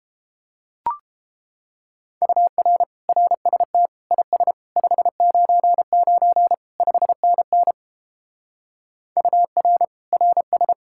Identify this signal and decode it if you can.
Morse code, heard as a keyed tone, about 33 wpm, 700 Hz. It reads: UR RST IS 599 5NN UR RS